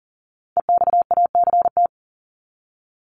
Morse code, heard as a keyed tone, about 40 wpm, 700 Hz. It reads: EXACT
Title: LY Suffix - Top 100 Words with Base Words 40wpm